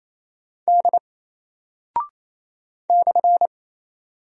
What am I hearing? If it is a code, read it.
Morse code, heard as a keyed tone, about 27 wpm, 700 Hz. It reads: D /
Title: Single Character (Letter:Number:Punctuation:Prosign) 28wpm